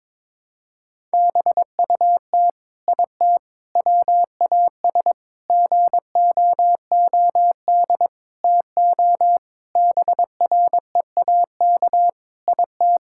Morse code, heard as a keyed tone, about 22 wpm, 700 Hz. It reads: BUT IT WAS GOOD TO BREAK IT
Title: Sentences from Top 1000 Words - Review Part 6 - 22wpm